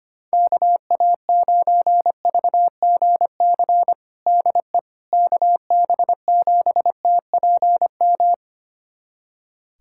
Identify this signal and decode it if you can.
Morse code, heard as a keyed tone, about 25 wpm, 700 Hz. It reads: KA9VGC DE KB7TPM
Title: USA Callsign DE Callsign - 6 Characters - 25wpm